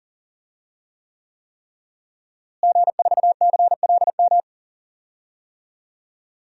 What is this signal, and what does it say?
Morse code, heard as a keyed tone, about 40 wpm, 700 Hz. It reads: G4CLM
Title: DXCC-100 Call Signs - 5 Characters - Repeated 3x - 40wpm